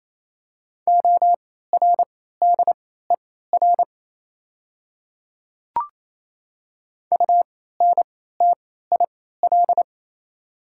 Morse code, heard as a keyed tone, about 28 wpm, spaced as for 16 wpm, 690 Hz. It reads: ORDER UNTIL
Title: 3x Character Spacing - Top 500 Words - 28wpm